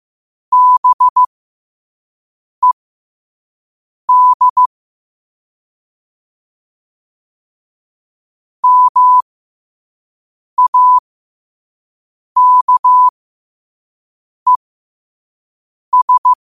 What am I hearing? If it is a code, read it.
Morse code, heard as a keyed tone, about 15 wpm, spaced as for 5 wpm, 1000 Hz. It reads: BED MAKES
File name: qotdfiles / 5wpm